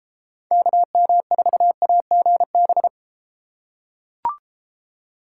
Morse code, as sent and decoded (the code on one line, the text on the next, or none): -.- -- ....- .- --. -...
KM4AGB